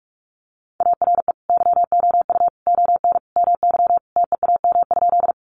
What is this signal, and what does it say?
Morse code, heard as a keyed tone, about 44 wpm, 700 Hz. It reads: ARE YOU ON MY TEAM?
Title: Sentences from Top 1500 Words 45wpm